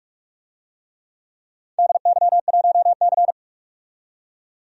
Morse code, heard as a keyed tone, about 46 wpm, 700 Hz. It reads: DY1C
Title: DXCC-100 Call Signs - 4 Characters - 45wpm